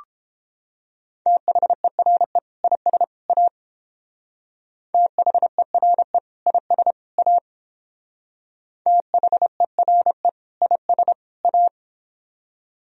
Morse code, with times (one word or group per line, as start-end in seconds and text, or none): 1.3-2.4 THERE
2.6-3.0 IS
3.3-3.5 A
4.9-6.2 THERE
6.5-6.9 IS
7.2-7.4 A
8.9-10.3 THERE
10.6-11.1 IS
11.4-11.7 A